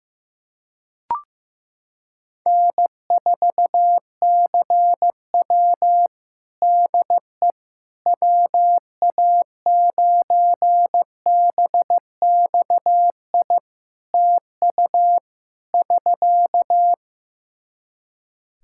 Morse code, heard as a keyed tone, about 15 wpm, 700 Hz. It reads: N4CW DE WA9BXI TU <SK>
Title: Practice QSO - Exchange 4 - Ending - 15wpm